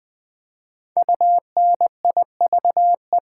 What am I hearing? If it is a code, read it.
Morse code, heard as a keyed tone, about 20 wpm, 700 Hz. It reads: UNIVE
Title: Build A Word - 5 Letters At A Time - 10-Letter Words - 20wpm